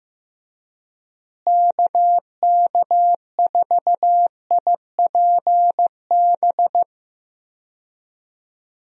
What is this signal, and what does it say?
Morse code, heard as a keyed tone, about 16 wpm, 700 Hz. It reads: KK4IPB
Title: USA Call Signs - 6 Characters - Repeated 2x - 15wpm